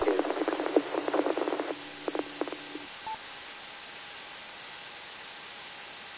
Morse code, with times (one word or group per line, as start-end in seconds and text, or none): none